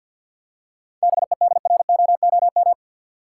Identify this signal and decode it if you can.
Morse code, heard as a keyed tone, about 50 wpm, 700 Hz. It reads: BEDROOM
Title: Seven Letter Words - 50wpm